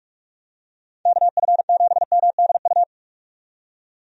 Morse code, heard as a keyed tone, about 46 wpm, 700 Hz. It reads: KF7MDU
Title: USA Call Signs - 6 Characters - 45wpm